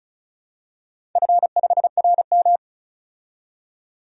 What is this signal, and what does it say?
Morse code, heard as a keyed tone, about 36 wpm, 700 Hz. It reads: F5RM
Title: DXCC-100 Call Signs - 4 Characters - 35wpm